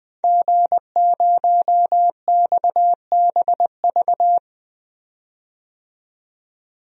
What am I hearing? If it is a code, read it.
Morse code, heard as a keyed tone, about 20 wpm, 700 Hz. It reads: G0XBV
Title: DXCC-100 Call Signs - 5 Characters - 20wpm